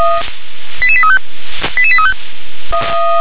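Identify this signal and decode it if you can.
Five-tone selective call (ZVEI), heard as quick runs of five tones, tones of about 70 ms. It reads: 88835 88835
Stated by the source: ZVEI five-tone selective call 88100, received on an SDR.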